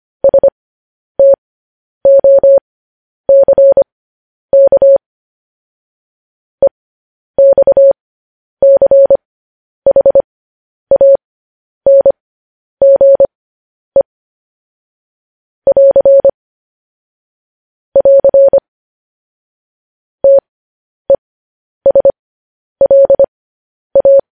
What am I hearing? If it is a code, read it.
Morse code, heard as a keyed tone, about 25 wpm, spaced as for 10 wpm, 550 Hz. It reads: STOCK EXCHANGE <AR> <AR> TESLA
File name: podcast / News headlines-1x-25x10